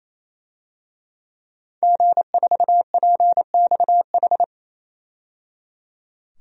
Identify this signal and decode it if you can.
Morse code, heard as a keyed tone, about 28 wpm, 700 Hz. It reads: G4PXH
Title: DXCC-100 Call Signs - 5 Characters - 28wpm